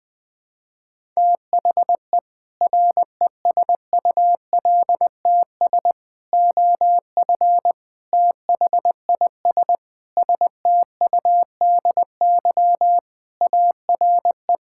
Morse code, heard as a keyed tone, about 20 wpm, 700 Hz. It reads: THE RESULTS OF THIS STUDY ARE
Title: Sentences with 5-Word Phrases - 20wpm